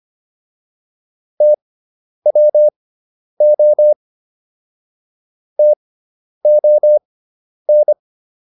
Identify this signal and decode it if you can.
Morse code, heard as a keyed tone, about 25 wpm, spaced as for 10 wpm, 600 Hz. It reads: TWO TON